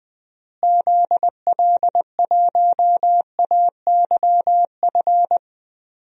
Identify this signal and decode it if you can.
Morse code, heard as a keyed tone, about 20 wpm, 700 Hz. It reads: ZL1AYF
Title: DXCC-100 Call Signs - 6 Characters - Repeated 3x - 20wpm